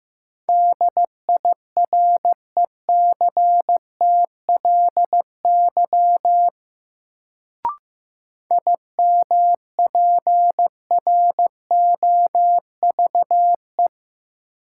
Morse code, heard as a keyed tone, about 15 wpm, 700 Hz. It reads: DIRECTLY IMPROVE
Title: Sentences from Top 800 Words - New Word Drill - 15wpm